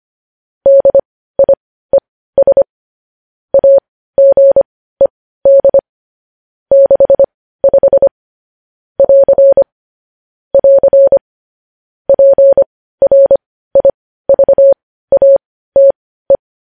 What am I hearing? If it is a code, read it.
Morse code, heard as a keyed tone, about 25 wpm, spaced as for 15 wpm, 550 Hz. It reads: DIES AGED 65 <AR> <AR> PRIVATE
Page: news headlines in morse code at 15 wpm